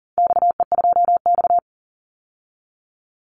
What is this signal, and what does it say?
Morse code, heard as a keyed tone, about 40 wpm, 700 Hz. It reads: XE2X
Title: Call Signs Repeated 3x - 40wpm